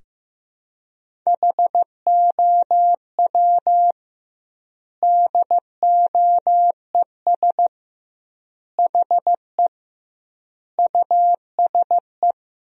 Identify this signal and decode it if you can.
Morse code, heard as a keyed tone, about 15 wpm, 700 Hz. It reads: HOW DOES HE USE
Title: 2x Word-Spacing - Sentences from Top 200 Words - 15wpm